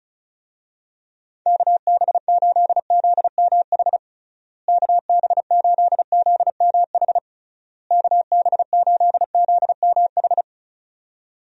Morse code, heard as a keyed tone, about 36 wpm, 700 Hz. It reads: KB8ZMH KB8ZMH KB8ZMH
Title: USA Call Signs - 6 Characters - Repeated 3x - 35wpm